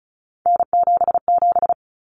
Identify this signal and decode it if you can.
Morse code, heard as a keyed tone, about 34 wpm, 700 Hz. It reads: N77
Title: Common Cut Numbers - 3-Digit 35wpm